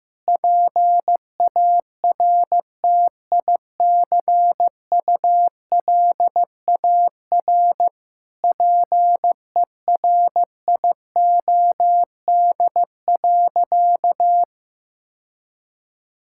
Morse code, heard as a keyed tone, about 16 wpm, 700 Hz. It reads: PARTICULAR PERIOD.
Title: Sentences from Top 1000 Words - Review Part 5 - 15wpm